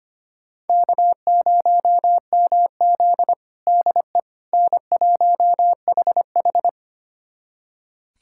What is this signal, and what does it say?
Morse code, heard as a keyed tone, about 25 wpm, 700 Hz. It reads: K0MZ DE N1HH
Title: USA Callsign DE Callsign - 4 Characters - 25wpm